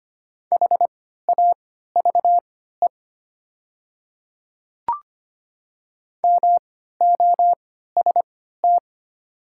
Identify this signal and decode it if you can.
Morse code, heard as a keyed tone, about 25 wpm, spaced as for 14 wpm, 700 Hz. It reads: HAVE MOST
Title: 3x Character Spacing - Top 100 Words - 25wpm